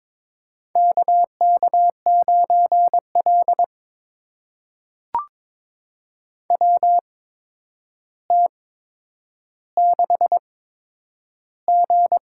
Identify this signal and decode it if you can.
Morse code, heard as a keyed tone, about 22 wpm, spaced as for 9 wpm, 700 Hz. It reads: KK9L WT6G
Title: USA Call Signs - 4 Characters - 8x Character Spacing - 22wpm